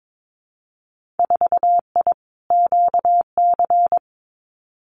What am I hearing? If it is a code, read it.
Morse code, heard as a keyed tone, about 22 wpm, 700 Hz. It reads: 4I QC
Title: Winter Field Day Exchanges Encore 22wpm